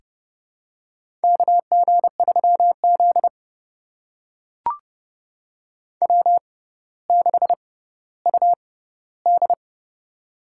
Morse code, heard as a keyed tone, about 31 wpm, spaced as for 16 wpm, 700 Hz. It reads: KG3Z W6UD
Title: USA Call Signs - 6x Character Spacing - 30wpm